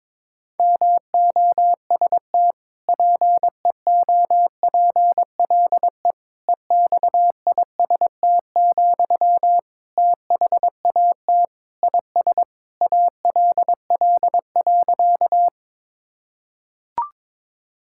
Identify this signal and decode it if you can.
Morse code, heard as a keyed tone, about 22 wpm, 700 Hz. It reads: MOST PEOPLE EXIST, THAT IS ALL.
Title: Sentences from Top 1000 Words - Review Part 5 - 22wpm